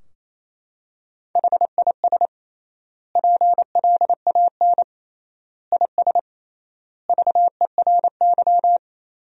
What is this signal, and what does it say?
Morse code, heard as a keyed tone, about 27 wpm, 700 Hz. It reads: HIS PLAN IS VERY
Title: 3x Word-Spacing - Sentences from Top 500 Words - 28wpm